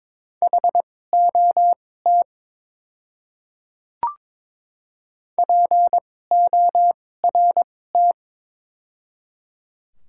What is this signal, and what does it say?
Morse code, heard as a keyed tone, about 21 wpm, spaced as for 16 wpm, 700 Hz. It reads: HOT PORT
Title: 2x Character Spacing - Top 200 Words - 22wpm